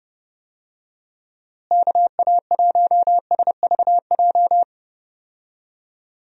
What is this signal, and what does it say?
Morse code, heard as a keyed tone, about 31 wpm, 700 Hz. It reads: KA1SVJ
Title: USA Call Signs - 6 Characters - Repeated 2x - 30wpm